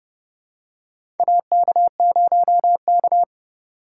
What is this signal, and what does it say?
Morse code, heard as a keyed tone, about 31 wpm, 700 Hz. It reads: AK0K
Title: USA Call Signs - 4 Characters - 2x Character Spacing - 30wpm